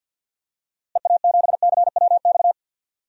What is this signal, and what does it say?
Morse code, heard as a keyed tone, about 53 wpm, 700 Hz. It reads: EA7CWX
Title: DXCC-100 Call Signs - 6 Characters - Repeated 2x - 50wpm